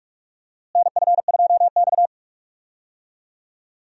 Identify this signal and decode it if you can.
Morse code, heard as a keyed tone, about 46 wpm, 700 Hz. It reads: NF2X
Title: USA Call Signs - 4 Characters - Repeated 2x - 45wpm